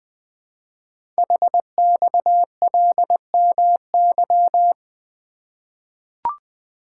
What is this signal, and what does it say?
Morse code, heard as a keyed tone, about 20 wpm, 700 Hz. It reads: HXLMY
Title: Code Groups - 5 Letters - 20wpm